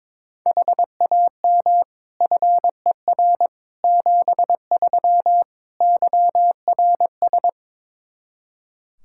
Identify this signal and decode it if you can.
Morse code, heard as a keyed tone, about 22 wpm, 700 Hz. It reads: HAM FER 73 YRS
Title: Practice QSO - Exchange 3 - More Info - 22wpm